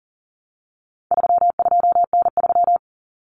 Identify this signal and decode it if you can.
Morse code, heard as a keyed tone, about 40 wpm, 700 Hz. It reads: 32N3
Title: Common Cut Numbers - 4-Digit 40wpm